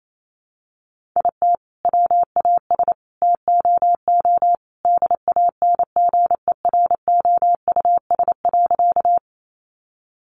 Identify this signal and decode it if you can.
Morse code, heard as a keyed tone, about 28 wpm, 700 Hz. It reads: IT WAS TOO DANGEROUS.
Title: Sentences from Top 1200 Words 28wpm